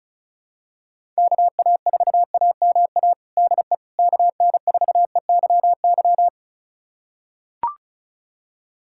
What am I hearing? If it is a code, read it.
Morse code, heard as a keyed tone, about 34 wpm, 700 Hz. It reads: KA4AMA DE KN4EYY
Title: USA Callsign DE Callsign 35wpm